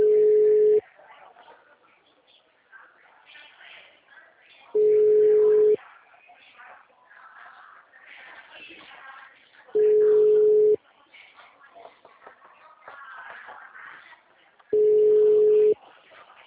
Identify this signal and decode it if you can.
Ringback tone (Europe, 425 Hz), with speech.